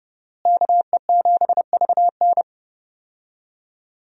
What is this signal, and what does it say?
Morse code, heard as a keyed tone, about 30 wpm, 700 Hz. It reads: KE7VN
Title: USA Call Signs - 5 Characters - Repeated 2x - 30wpm